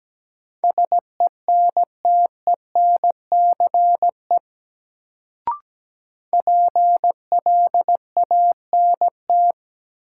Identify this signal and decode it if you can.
Morse code, heard as a keyed tone, about 17 wpm, 700 Hz. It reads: SENTENCE PLANT
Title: Sentences from Top 1000 Words - Review Part 1 - New Word Drill - 17wpm